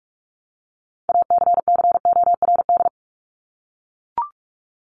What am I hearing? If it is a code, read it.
Morse code, heard as a keyed tone, about 44 wpm, 700 Hz. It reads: ACCORD
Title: ING Suffix - Top 100 Words with Base Words 45wpm